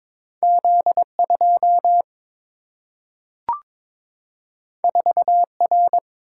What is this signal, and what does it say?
Morse code, heard as a keyed tone, about 22 wpm, 700 Hz. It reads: Z 2 4 R
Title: Single-Letter Single-Number 22wpm